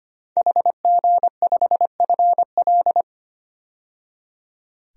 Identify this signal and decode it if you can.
Morse code, heard as a keyed tone, about 25 wpm, 690 Hz. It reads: HG5FL
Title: DXCC-100 Call Signs - 5 Characters - 25wpm